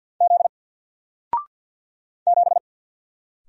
Morse code, heard as a keyed tone, about 48 wpm, 700 Hz. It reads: Z 7
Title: Single Letter-Number Alt 50wpm